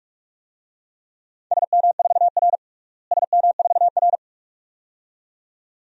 Morse code, heard as a keyed tone, about 42 wpm, 700 Hz. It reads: SM4R SM4R
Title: DXCC-100 Call Signs - 4 Characters - Repeated 2x - 45wpm